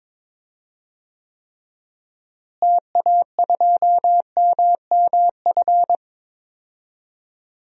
Morse code, heard as a keyed tone, about 22 wpm, 700 Hz. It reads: TA2MMF
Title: DXCC-100 Call Signs - 6 Characters - Repeated 3x - 22wpm